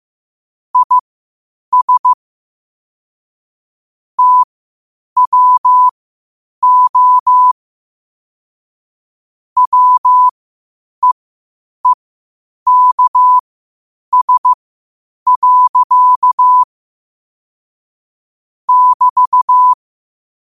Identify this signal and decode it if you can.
Morse code, heard as a keyed tone, about 15 wpm, spaced as for 8 wpm, 1000 Hz. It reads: IS TWO WEEKS. <BT>
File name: qotdfiles / 8wpm